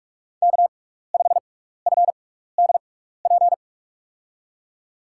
Morse code, heard as a keyed tone, about 46 wpm, spaced as for 16 wpm, 700 Hz. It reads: K5FDP
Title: USA Call Signs - 5 Characters - 6x Character Spacing - 45wpm